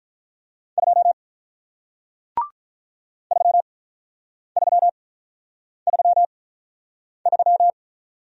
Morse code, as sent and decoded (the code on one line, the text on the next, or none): ..--- / ...-- ...-- ...-- ...--
2 3333